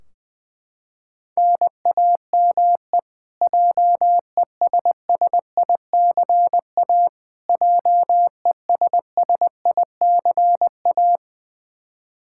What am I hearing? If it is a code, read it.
Morse code, heard as a keyed tone, about 20 wpm, 700 Hz. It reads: NAME JESSICA JESSICA
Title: Practice QSO - Exchange 1 - Beginning - 20wpm